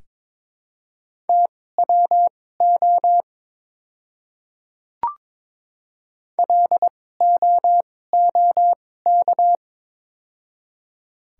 Morse code, heard as a keyed tone, about 22 wpm, spaced as for 16 wpm, 700 Hz. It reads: TWO LOOK
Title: 2x Character Spacing - Top 200 Words - 22wpm